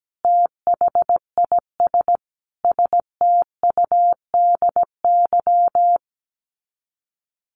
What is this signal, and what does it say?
Morse code, heard as a keyed tone, about 17 wpm, 700 Hz. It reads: THIS STUDY
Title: Sets of 5 Words 17wpm